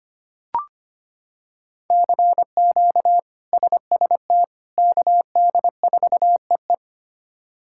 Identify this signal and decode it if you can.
Morse code, heard as a keyed tone, about 25 wpm, 700 Hz. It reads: CQ SST KD4EE
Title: K1USN SST QSO Practice - 25wpm